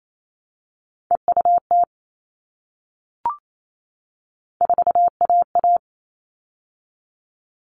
Morse code, heard as a keyed tone, about 28 wpm, 700 Hz. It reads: EUT 4AA
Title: All Cut Numbers - 3-Digit 28wpm